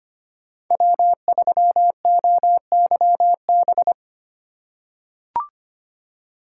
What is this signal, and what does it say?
Morse code, heard as a keyed tone, about 25 wpm, 700 Hz. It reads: W3OYB
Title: USA Call Signs - 5 Characters - 2x Character Spacing - 25wpm